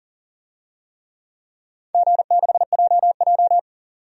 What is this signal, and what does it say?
Morse code, heard as a keyed tone, about 39 wpm, 700 Hz. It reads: G6JJ
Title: DXCC-100 Call Signs - 4 Characters - 40wpm